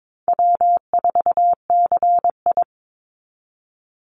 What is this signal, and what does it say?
Morse code, heard as a keyed tone, about 22 wpm, 700 Hz. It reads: W4CI
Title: Easy Call Signs Repeated 2x - 22wpm